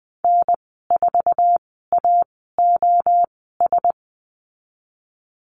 Morse code, heard as a keyed tone, about 20 wpm, spaced as for 14 wpm, 700 Hz. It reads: N4AOS
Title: 069 Lesson 12g - Letter d - Call Signs 20wpm